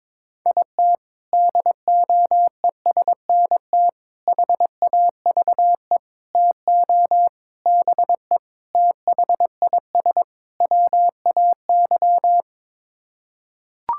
Morse code, heard as a keyed tone, about 22 wpm, 700 Hz. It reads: IT DOESNT HAVE TO BE THIS WAY
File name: practice / 22wpm / Sets of 7 Words - 22wpm